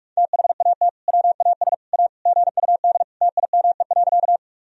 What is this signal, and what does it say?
Morse code, heard as a keyed tone, about 42 wpm, 700 Hz. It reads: THAT WAS A GUD TIME.
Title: Sentences from Top 200 Words and Abbreviations 45wpm